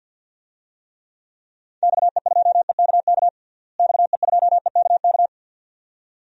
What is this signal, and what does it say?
Morse code, heard as a keyed tone, about 52 wpm, 700 Hz. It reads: XE2EKK XE2EKK
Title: DXCC-100 Call Signs - 6 Characters - Repeated 2x - 50wpm